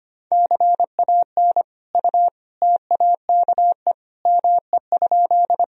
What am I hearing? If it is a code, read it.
Morse code, heard as a keyed tone, about 25 wpm, 700 Hz. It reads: CAN U TAKE ME?
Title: Sentences from Top 200 Words and Abbreviations 25wpm